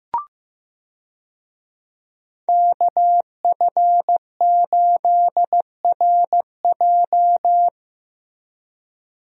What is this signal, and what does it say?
Morse code, heard as a keyed tone, about 15 wpm, 700 Hz. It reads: KF8RJ
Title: USA Call Signs - 5 Characters - Repeated 3x - 15wpm